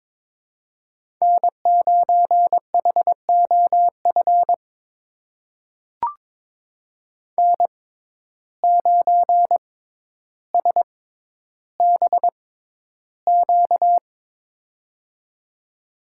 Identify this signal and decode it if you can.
Morse code, heard as a keyed tone, about 22 wpm, spaced as for 11 wpm, 700 Hz. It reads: N9HOF N9SBQ
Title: USA Call Signs - 5 Characters - 6x Character Spacing - 22wpm